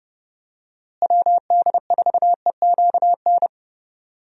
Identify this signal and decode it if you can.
Morse code, heard as a keyed tone, about 32 wpm, 700 Hz. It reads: WD4EQN
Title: USA Call Signs - 6 Characters - Repeated 3x - 30wpm